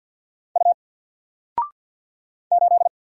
Morse code, heard as a keyed tone, about 50 wpm, 700 Hz. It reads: U 8
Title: Single Character (Letter:Number:Punctuation:Prosign) 50wpm